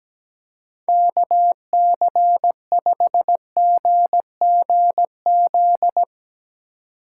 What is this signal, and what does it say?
Morse code, heard as a keyed tone, about 17 wpm, 700 Hz. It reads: KC5GGZ
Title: Speed Racing - USA 2x3 Call Signs 17wpm